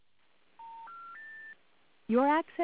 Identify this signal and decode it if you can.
Special information tone, with speech.